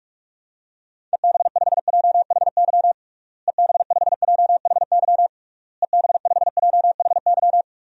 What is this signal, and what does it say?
Morse code, heard as a keyed tone, about 46 wpm, 700 Hz. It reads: EB5JHY EB5JHY EB5JHY
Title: DXCC-100 Call Signs - 6 Characters - Repeated 3x - 45wpm